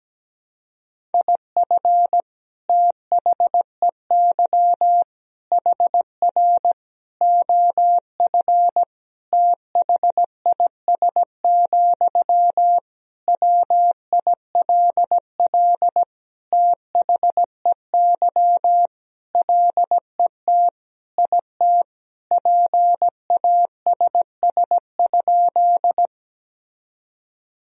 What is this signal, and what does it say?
Morse code, heard as a keyed tone, about 17 wpm, 700 Hz. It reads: IF THEY HR OF THIS, WILL THEY LET IT PASS?
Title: Sentences from Top 400 Words and Abbreviations 17wpm